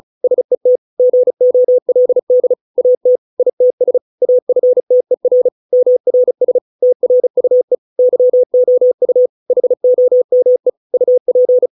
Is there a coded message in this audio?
Yes: Morse code from 0.2 to 11.7 s, heard as a keyed tone, about 34 wpm, 500 Hz. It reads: SET GOLD AT ITS AFTER MRS TRUE YOU HOME UP